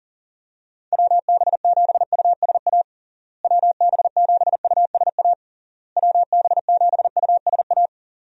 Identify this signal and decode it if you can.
Morse code, heard as a keyed tone, about 40 wpm, 700 Hz. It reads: WB7USA WB7USA WB7USA
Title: USA Call Signs - 6 Characters - Repeated 3x - 40wpm